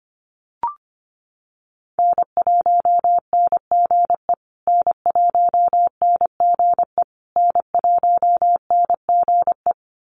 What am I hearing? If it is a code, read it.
Morse code, heard as a keyed tone, about 25 wpm, 700 Hz. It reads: N1NGE N1NGE N1NGE
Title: USA Call Signs Repeated 3x 25wpm